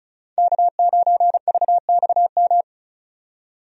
Morse code, heard as a keyed tone, about 36 wpm, 700 Hz. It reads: K9VXM
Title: USA Call Signs - 5 Characters - 35wpm